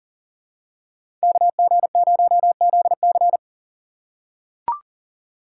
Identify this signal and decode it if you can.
Morse code, heard as a keyed tone, about 40 wpm, 700 Hz. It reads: KG0ZC
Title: USA Call Signs - 5 Characters - 2x Character Spacing - 40wpm